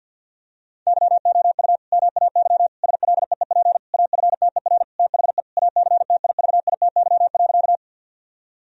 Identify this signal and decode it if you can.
Morse code, heard as a keyed tone, about 50 wpm, 700 Hz. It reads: YOU MAY SLEEP AFTER THE ACTIVITY.
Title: Sentences from Top 1000 Words - Review Part 4 - 50wpm